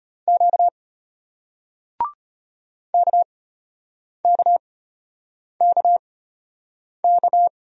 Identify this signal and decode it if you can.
Morse code, heard as a keyed tone, about 33 wpm, spaced as for 8 wpm, 700 Hz. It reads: Q KKKK